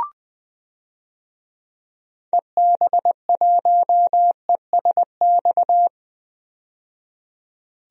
Morse code, heard as a keyed tone, about 21 wpm, 700 Hz. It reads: EB1ESX
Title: DXCC-100 Call Signs - 6 Characters - 20wpm